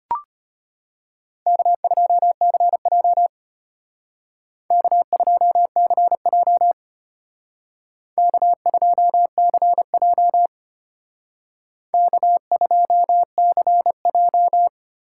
Morse code, heard as a keyed tone, about 32 wpm, 700 Hz. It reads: K2CJ K2CJ K2CJ K2CJ